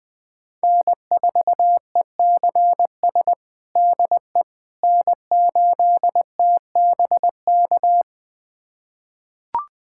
Morse code, heard as a keyed tone, about 20 wpm, 700 Hz. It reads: N4ECS DE N8TBK
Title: USA Callsign DE Callsign - 5 Characters - 20wpm